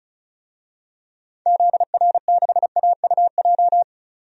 Morse code, heard as a keyed tone, about 36 wpm, 700 Hz. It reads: ZR6AUJ
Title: DXCC-100 Call Signs - 6 Characters - 35wpm